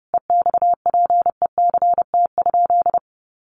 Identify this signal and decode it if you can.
Morse code, heard as a keyed tone, about 30 wpm, 700 Hz. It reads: EXPECT?
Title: Sentences from Top 1100 Words 30wpm